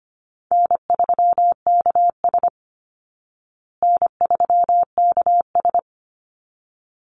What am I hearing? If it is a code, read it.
Morse code, heard as a keyed tone, about 25 wpm, 700 Hz. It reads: N3KS N3KS
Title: Easy Call Signs Repeated 2x - 25wpm